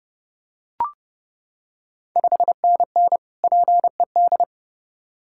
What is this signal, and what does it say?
Morse code, heard as a keyed tone, about 30 wpm, 700 Hz. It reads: 5NN PED
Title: Ontario QSO Party Exchange 30wpm